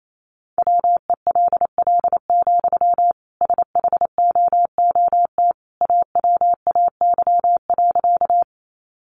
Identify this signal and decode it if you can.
Morse code, heard as a keyed tone, about 28 wpm, 700 Hz. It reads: WELL, SHOOT AWAY.
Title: Sentences from Top 1200 Words 28wpm